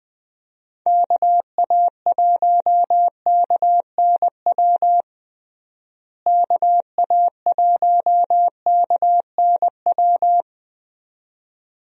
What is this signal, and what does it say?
Morse code, heard as a keyed tone, about 20 wpm, 700 Hz. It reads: KA1KNW KA1KNW
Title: USA Call Signs - 6 Characters - Repeated 2x - 20wpm